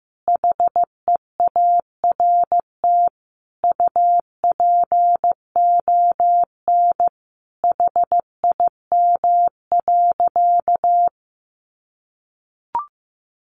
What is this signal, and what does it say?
Morse code, heard as a keyed tone, about 15 wpm, 700 Hz. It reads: HEART UPON HIM.
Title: Sentences from Top 1200 Words 15wpm